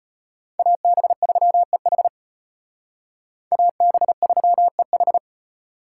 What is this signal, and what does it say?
Morse code, heard as a keyed tone, about 38 wpm, 700 Hz. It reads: AB3EH AB3EH